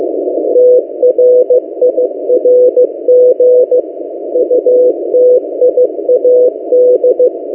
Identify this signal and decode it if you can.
Morse code, heard as a keyed tone, about 15 wpm, 500 Hz. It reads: TRIRG UTIAD
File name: M08a- -1355utc-8096khz